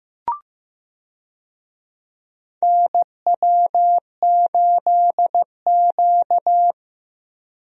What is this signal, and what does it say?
Morse code, heard as a keyed tone, about 15 wpm, 700 Hz. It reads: NW8Q
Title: USA Call Signs - 4 Characters - Repeated 3x - 15wpm